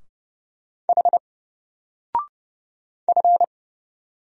Morse code, heard as a keyed tone, about 29 wpm, 700 Hz. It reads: H F
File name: Single-Letter-Number-30wpm